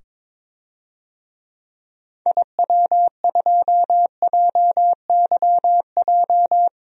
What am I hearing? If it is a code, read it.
Morse code, heard as a keyed tone, about 22 wpm, 700 Hz. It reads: IW2JYJ
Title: DXCC-100 Call Signs - 6 Characters - 22wpm